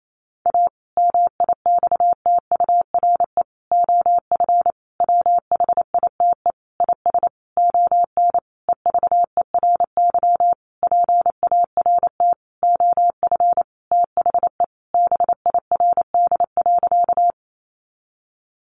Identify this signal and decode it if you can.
Morse code, heard as a keyed tone, about 28 wpm, 700 Hz. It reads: A MIXTURE OF WHITE IS ON EVERY PART OF THE BIRD.
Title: Sentences from Top 1500 Words 28wpm